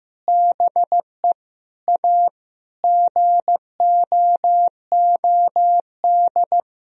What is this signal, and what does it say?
Morse code, heard as a keyed tone, about 15 wpm, 700 Hz. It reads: BE A GOOD
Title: Sentences from Top 1000 Words - Review Part 4 - 15wpm